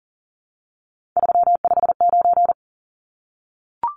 Morse code, heard as a keyed tone, about 40 wpm, 700 Hz. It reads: 359
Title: Numbers - 3-Digit 40wpm